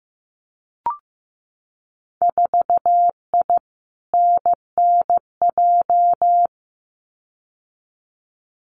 Morse code, heard as a keyed tone, about 15 wpm, 700 Hz. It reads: 4I NNJ
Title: Winter Field Day Exchanges 15wpm